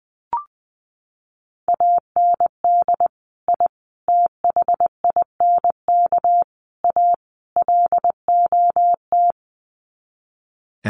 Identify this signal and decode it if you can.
Morse code, heard as a keyed tone, about 20 wpm, 700 Hz. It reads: AND I THINK A LOT
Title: Sets of 5 Words Encore2 20wpm